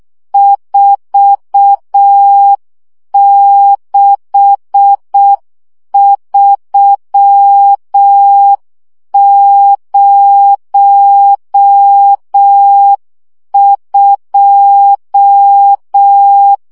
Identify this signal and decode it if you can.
Morse code, heard as a keyed tone, about 6 wpm, 800 Hz. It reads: 46302
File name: morze vizsga 2011-2 5